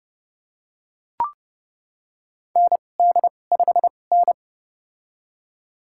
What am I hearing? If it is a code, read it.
Morse code, heard as a keyed tone, about 30 wpm, spaced as for 22 wpm, 700 Hz. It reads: ND5N
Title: USA Call Signs - 4 Characters - 2x Character Spacing - 30wpm